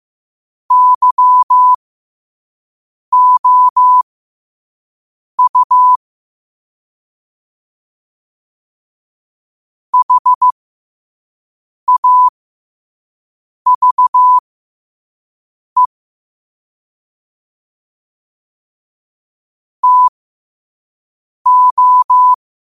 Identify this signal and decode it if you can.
Morse code, heard as a keyed tone, about 15 wpm, spaced as for 5 wpm, 1000 Hz. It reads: YOU HAVE TO
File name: qotdfiles / 5wpm